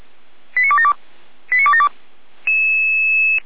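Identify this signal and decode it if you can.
Five-tone selective call (ZVEI), heard as quick runs of five tones, tones of about 70 ms. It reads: 89272 89272
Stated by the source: ZVEI five-tone selective call received by SDR, archived by Feuerwehr Orlamünde.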